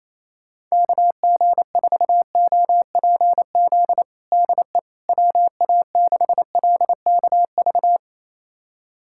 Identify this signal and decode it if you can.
Morse code, heard as a keyed tone, about 28 wpm, 700 Hz. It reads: KG4OPZ DE WA6LKV
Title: USA Callsign DE Callsign - 6 Characters - 28wpm